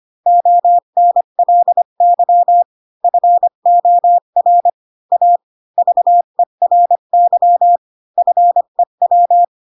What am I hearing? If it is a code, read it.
Morse code, heard as a keyed tone, about 25 wpm, 700 Hz. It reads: ONLY FOR A VERY FEW
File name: Pinocchio / Pinocchio ch5 20 25 ik1hge